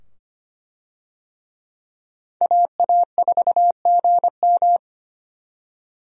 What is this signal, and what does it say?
Morse code, heard as a keyed tone, about 25 wpm, 700 Hz. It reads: AA4GM
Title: USA Call Signs - 5 Characters - 25wpm